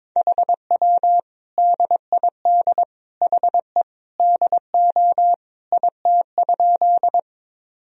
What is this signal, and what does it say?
Morse code, heard as a keyed tone, about 22 wpm, 700 Hz. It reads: HW DID HE DO IT?
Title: Sentences from Top 100 Words and Abbreviations 22wpm